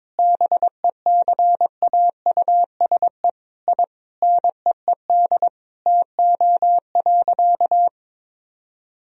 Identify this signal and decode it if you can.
Morse code, heard as a keyed tone, about 22 wpm, 700 Hz. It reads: BECAUSE I NEED TO.
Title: Sentences from Top 1000 Words - Review Part 3 - 22wpm